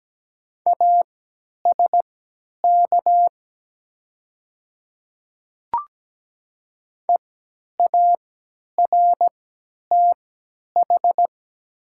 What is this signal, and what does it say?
Morse code, heard as a keyed tone, about 17 wpm, spaced as for 10 wpm, 700 Hz. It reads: ASK EARTH
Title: 3x Character Spacing - Top 200 Words - 17wpm